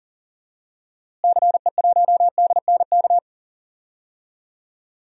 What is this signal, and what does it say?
Morse code, heard as a keyed tone, about 37 wpm, 700 Hz. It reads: CE1DNK
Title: DXCC-100 Call Signs - 6 Characters - Repeated 3x - 40wpm